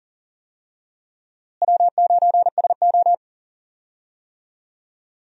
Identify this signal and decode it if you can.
Morse code, heard as a keyed tone, about 38 wpm, 700 Hz. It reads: W9SO
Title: USA Call Signs - 4 Characters - Repeated 3x - 40wpm